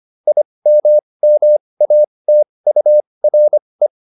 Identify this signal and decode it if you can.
Morse code, heard as a keyed tone, about 25 wpm, spaced as for 20 wpm, 600 Hz. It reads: IMMATURE